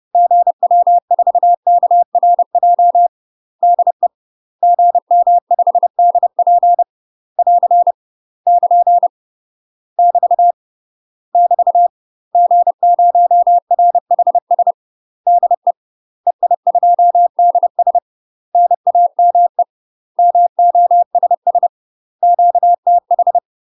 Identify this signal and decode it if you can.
Morse code, heard as a keyed tone, about 30 wpm, 700 Hz. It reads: GW4KRJ DE GM5DP <AR> <KN> <BT> <BT> G0RHS DE EI2DS NAME MOSS QTH
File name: UK MASTER QSOs TRAINING / 30WPM / UK MASTER QSOs 0001